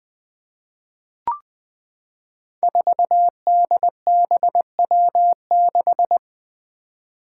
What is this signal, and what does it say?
Morse code, heard as a keyed tone, about 20 wpm, 700 Hz. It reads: 4DBW6
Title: Five Challenging Letter-Number Combinations 20wpm